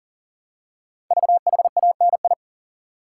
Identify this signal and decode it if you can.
Morse code, heard as a keyed tone, about 40 wpm, 700 Hz. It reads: VHANI
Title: Code Groups - 5 Letters - 40wpm